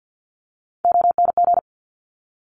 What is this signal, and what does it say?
Morse code, heard as a keyed tone, about 50 wpm, 700 Hz. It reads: ONG
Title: Top Three Letter Combinations in Words 50wpm